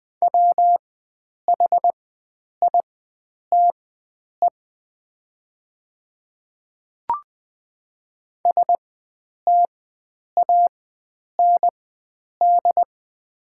Morse code, heard as a keyed tone, about 20 wpm, spaced as for 9 wpm, 690 Hz. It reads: WHITE STAND
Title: 4x Character Spacing - Top 300 Words - 20wpm